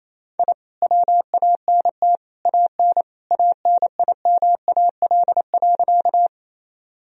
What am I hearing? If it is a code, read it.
Morse code, heard as a keyed tone, about 27 wpm, 700 Hz. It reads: I WANT AN ANIMAL.
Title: Sentences from Top 1000 Words - Review Part 1 - 28wpm